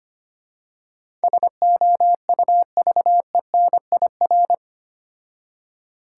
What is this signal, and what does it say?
Morse code, heard as a keyed tone, about 25 wpm, 700 Hz. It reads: SOUVENIR
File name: practice / 25wpm / Difficult Words with Hints - 25wpm